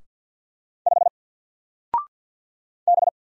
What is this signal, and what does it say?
Morse code, heard as a keyed tone, about 48 wpm, 700 Hz. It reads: 5 B
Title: Single Character (Letter:Number:Punctuation:Prosign) 50wpm